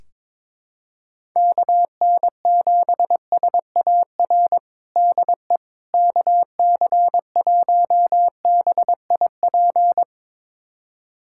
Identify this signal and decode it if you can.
Morse code, heard as a keyed tone, about 22 wpm, 700 Hz. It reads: KN7SAR DE KC1BIP
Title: USA Callsign DE Callsign 22wpm